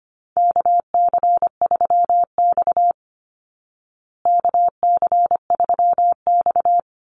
Morse code, heard as a keyed tone, about 25 wpm, 700 Hz. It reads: KC3X KC3X
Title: Call Signs Repeated 3x - 25wpm